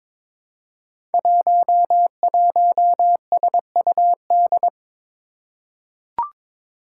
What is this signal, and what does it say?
Morse code, heard as a keyed tone, about 22 wpm, 700 Hz. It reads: 11SUD
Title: Five Challenging Letter-Number Combinations 22wpm